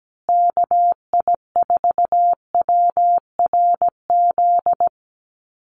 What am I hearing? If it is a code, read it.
Morse code, heard as a keyed tone, about 17 wpm, 700 Hz. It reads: KI4WRZ
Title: USA Call Signs Repeated 3x 17wpm